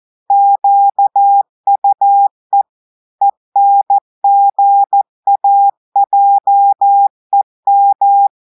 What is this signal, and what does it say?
Morse code, heard as a keyed tone, about 14 wpm, 800 Hz. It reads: QUE ENGAJEM
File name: audios / portugais / portugais texte en clair 2 vitesse 14 mots minutes